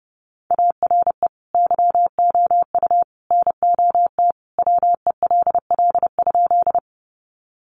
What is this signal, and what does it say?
Morse code, heard as a keyed tone, about 30 wpm, 700 Hz. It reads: ARE YOU NOT WELL?
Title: Sentences from Top 600 Words 30wpm